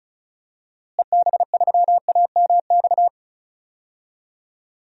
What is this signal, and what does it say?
Morse code, heard as a keyed tone, about 36 wpm, 700 Hz. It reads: EB3AMX
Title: DXCC-100 Call Signs - 6 Characters - Repeated 3x - 35wpm